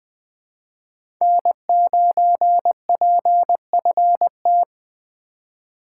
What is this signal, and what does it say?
Morse code, heard as a keyed tone, about 20 wpm, 700 Hz. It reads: N9PFT
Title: USA Call Signs - 5 Characters - 4x Character Spacing - 20wpm